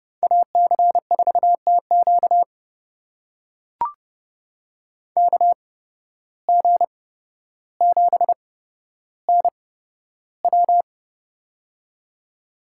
Morse code, heard as a keyed tone, about 30 wpm, spaced as for 13 wpm, 700 Hz. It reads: AC4TQ KG7NW
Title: USA Call Signs - 5 Characters - 8x Character Spacing - 30wpm